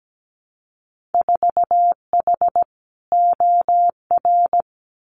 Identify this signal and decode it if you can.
Morse code, heard as a keyed tone, about 17 wpm, 700 Hz. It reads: 4H OR
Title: Winter Field Day Exchanges Encore 17wpm